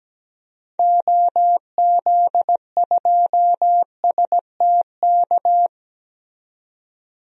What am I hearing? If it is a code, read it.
Morse code, heard as a keyed tone, about 17 wpm, 700 Hz. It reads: OZ2STK
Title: DXCC-100 Call Signs - 6 Characters - 17wpm